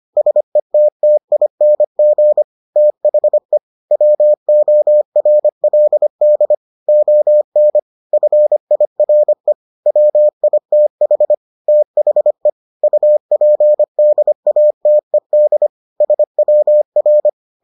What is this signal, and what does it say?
Morse code, heard as a keyed tone, about 25 wpm, 600 Hz. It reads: SETTING THE WORLD ON FIRE WITH THE UPDATED SWR